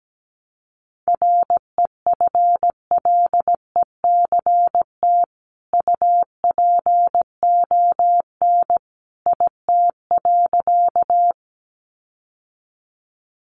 Morse code, heard as a keyed tone, about 17 wpm, 700 Hz. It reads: REFLECT UPON IT.
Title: Sentences from Top 1500 Words 17wpm